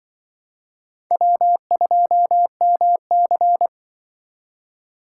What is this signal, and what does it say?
Morse code, heard as a keyed tone, about 25 wpm, 700 Hz. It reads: W2MC
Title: Speed Racing - USA 1x2 Call Signs 20wpm